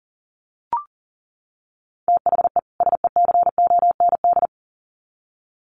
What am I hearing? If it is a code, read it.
Morse code, heard as a keyed tone, about 40 wpm, 700 Hz. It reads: THE SECOND
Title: Sets of 2 Words 40wpm